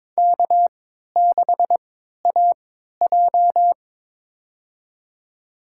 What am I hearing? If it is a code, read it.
Morse code, heard as a keyed tone, about 22 wpm, spaced as for 13 wpm, 700 Hz. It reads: K6AJ
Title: USA Call Signs - 4 Characters - 3x Character Spacing - 22wpm